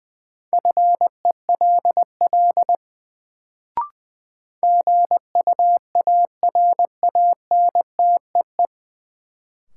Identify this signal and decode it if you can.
Morse code, heard as a keyed tone, about 20 wpm, 700 Hz. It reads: FELL GUARANTEE
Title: Sentences from Top 1600 Words - New Word Drill - 20wpm